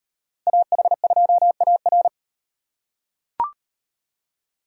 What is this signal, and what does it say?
Morse code, heard as a keyed tone, about 38 wpm, 700 Hz. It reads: AH2AR